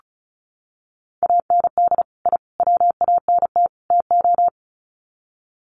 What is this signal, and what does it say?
Morse code, heard as a keyed tone, about 34 wpm, 700 Hz. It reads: AND I WANT TO
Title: Sets of 4 Words 35wpm